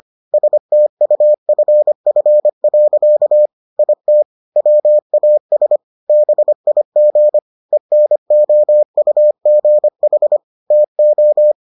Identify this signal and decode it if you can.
Morse code, heard as a keyed tone, about 25 wpm, 600 Hz. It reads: STUFF. IT WAS BIG ENOUGH TO